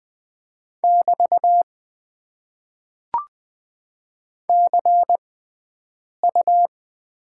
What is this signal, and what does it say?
Morse code, heard as a keyed tone, about 20 wpm, spaced as for 7 wpm, 700 Hz. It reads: <BT> CU
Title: Top 90 QSO Elements - 6x Character Spacing - 20wpm